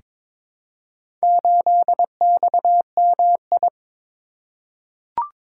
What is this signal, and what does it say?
Morse code, heard as a keyed tone, about 22 wpm, 700 Hz. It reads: 8XMI
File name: practice / 22wpm / Four Letter-Number Combinations - 22wpm